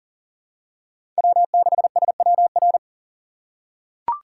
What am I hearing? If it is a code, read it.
Morse code, heard as a keyed tone, about 40 wpm, 700 Hz. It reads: W6SWR
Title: USA Call Signs - 5 Characters - 3x Character Spacing - 40wpm